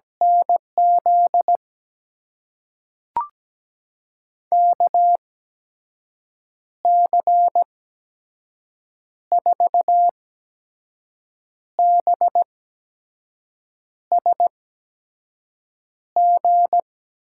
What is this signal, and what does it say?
Morse code, heard as a keyed tone, about 17 wpm, spaced as for 5 wpm, 700 Hz. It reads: NZ KC4BSG
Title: USA Call Signs - 6 Characters - 8x Character Spacing - 17wpm